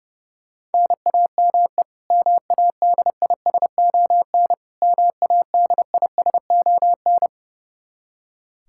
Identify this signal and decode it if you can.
Morse code, heard as a keyed tone, about 32 wpm, 700 Hz. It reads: NAME MADISON MADISON
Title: K1USN SST QSO Practice 30wpm